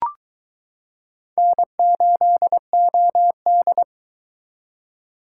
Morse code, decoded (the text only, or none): N8OD